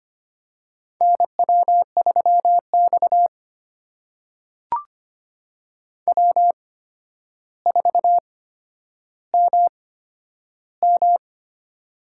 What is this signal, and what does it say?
Morse code, heard as a keyed tone, about 25 wpm, spaced as for 11 wpm, 700 Hz. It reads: NW3X W4MM
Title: USA Call Signs - 4 Characters - 8x Character Spacing - 25wpm